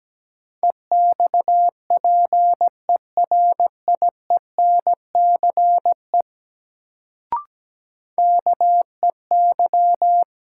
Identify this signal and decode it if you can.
Morse code, heard as a keyed tone, about 17 wpm, 700 Hz. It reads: EXPERIENCE KEY
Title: Sentences from Top 600 Words - New Word Drill - 17wpm